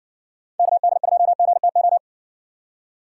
Morse code, heard as a keyed tone, about 57 wpm, 700 Hz. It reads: KD2GTO